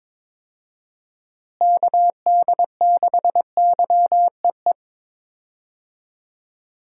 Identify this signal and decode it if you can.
Morse code, heard as a keyed tone, about 21 wpm, 700 Hz. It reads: KD6YEE
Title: USA Call Signs - 6 Characters - Repeated 3x - 22wpm